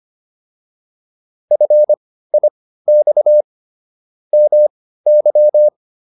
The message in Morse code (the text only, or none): FIX MY